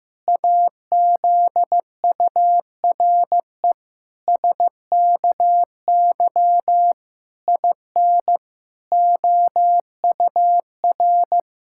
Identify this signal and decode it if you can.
Morse code, heard as a keyed tone, about 15 wpm, 700 Hz. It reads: AZURE SKY IN OUR
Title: Pangram Challenge v2 - 15wpm